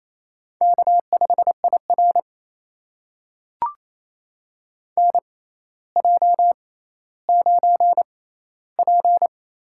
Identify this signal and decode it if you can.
Morse code, heard as a keyed tone, about 28 wpm, spaced as for 14 wpm, 700 Hz. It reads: K5IR NJ9P
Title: USA Call Signs - 6x Character Spacing - 28wpm